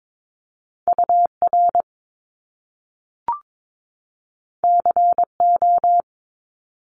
Morse code, heard as a keyed tone, about 22 wpm, 700 Hz. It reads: UR CO
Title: Top Two Letter Combinations in Words 22wpm